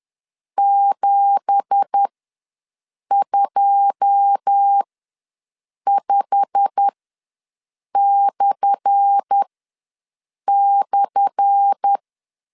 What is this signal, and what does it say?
Morse code, heard as a keyed tone, about 10 wpm, spaced as for 6 wpm, 800 Hz. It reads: 725//
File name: LEZIONI / 10 SIGNS / 10A SLASH ws 10wpm 4121